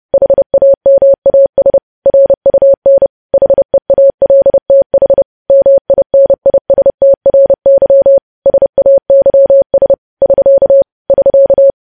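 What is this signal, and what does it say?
Morse code, heard as a keyed tone, about 30 wpm, 550 Hz. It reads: HAMAS RUN HEALTH MINISTRY SAYS <SK> <SK>
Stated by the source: Morse code news headlines sent at 30 wpm.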